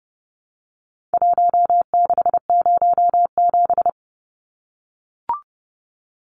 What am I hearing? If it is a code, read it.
Morse code, heard as a keyed tone, about 30 wpm, 700 Hz. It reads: 1607
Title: Numbers - Contest Style - 4-Digit 30wpm